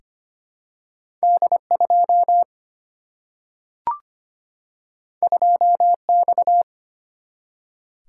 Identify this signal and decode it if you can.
Morse code, heard as a keyed tone, about 25 wpm, 700 Hz. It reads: D2 2X
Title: Single-Letter Single-Number 25wpm